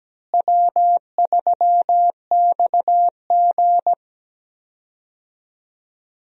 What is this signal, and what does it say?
Morse code, heard as a keyed tone, about 17 wpm, 700 Hz. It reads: W3XG
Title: USA Call Signs - 4 Characters - Repeated 3x - 17wpm